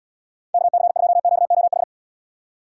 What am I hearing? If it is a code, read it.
Morse code, heard as a keyed tone, about 78 wpm, 700 Hz. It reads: KC2QOH